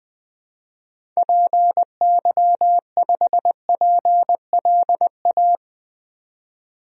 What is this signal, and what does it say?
Morse code, heard as a keyed tone, about 20 wpm, 700 Hz. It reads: PY5PLA
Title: DXCC-100 Call Signs - 6 Characters - 20wpm